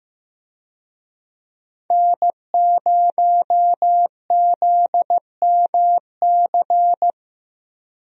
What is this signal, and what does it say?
Morse code, heard as a keyed tone, about 15 wpm, 700 Hz. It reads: N0ZMC
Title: USA Call Signs - 5 Characters - Repeated 3x - 15wpm